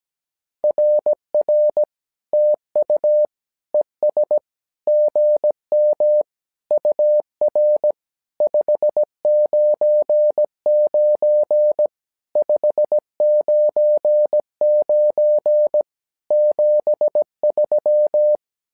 Morse code, heard as a keyed tone, about 17 wpm, 600 Hz. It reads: RR TU ES GM UR 599 599 73
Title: SOTA QSO Practice 17wpm